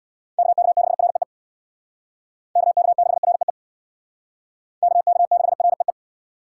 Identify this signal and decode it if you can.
Morse code, heard as a keyed tone, about 61 wpm, 700 Hz. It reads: KK6REE KK6REE KK6REE
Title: Speed Racing - USA 2x3 Call Signs 50wpm